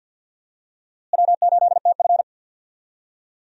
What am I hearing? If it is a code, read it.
Morse code, heard as a keyed tone, about 53 wpm, 700 Hz. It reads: W8TF
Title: USA Call Signs - 4 Characters - Repeated 3x - 50wpm